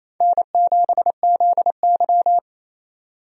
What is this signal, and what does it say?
Morse code, heard as a keyed tone, about 28 wpm, 700 Hz. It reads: N7ZY
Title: USA Call Signs - 4 Characters - 28wpm